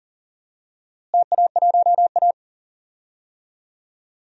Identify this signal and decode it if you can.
Morse code, heard as a keyed tone, about 40 wpm, 700 Hz. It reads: TA1A
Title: DXCC-100 Call Signs - 4 Characters - Repeated 2x - 40wpm